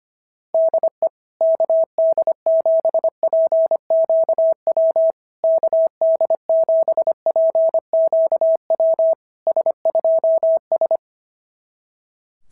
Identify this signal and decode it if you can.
Morse code, heard as a keyed tone, about 25 wpm, 650 Hz. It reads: DE KD7PQW KD7PQW S2S
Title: SOTA QSO Practice 25wpm